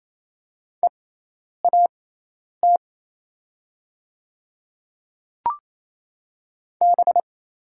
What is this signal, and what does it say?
Morse code, heard as a keyed tone, about 28 wpm, spaced as for 10 wpm, 700 Hz. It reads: EAT B
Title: 6x Character Spacing - Top 500 Words - 28wpm